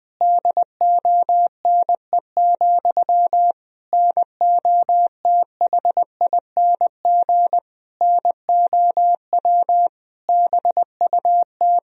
Morse code, heard as a keyed tone, about 21 wpm, 700 Hz. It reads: DONE, NOTHING NOW BUT
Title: Sentences from Top 1000 Words - Review Part 4 - 20wpm